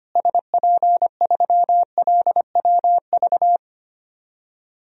Morse code, heard as a keyed tone, about 25 wpm, 700 Hz. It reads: SP3LWV
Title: DXCC-100 Call Signs - 6 Characters - Repeated 2x - 25wpm